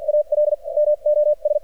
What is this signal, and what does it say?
Morse code, heard as a keyed tone, about 53 wpm, spaced as for 38 wpm, 600 Hz, with radio noise. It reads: ALMOS
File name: archived / CRSnet